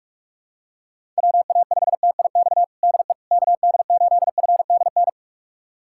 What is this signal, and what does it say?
Morse code, heard as a keyed tone, about 46 wpm, 700 Hz. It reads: WA5TIX DE KD8FDN
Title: USA Callsign DE Callsign - 6 Characters - 45wpm